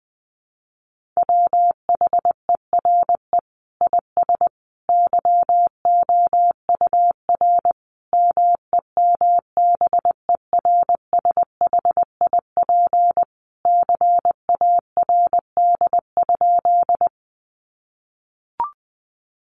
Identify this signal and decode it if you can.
Morse code, heard as a keyed tone, about 20 wpm, 700 Hz. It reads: WHERE IS YOUR MEMBERSHIP CARD?
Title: Sentences from Top 1400 Words 20wpm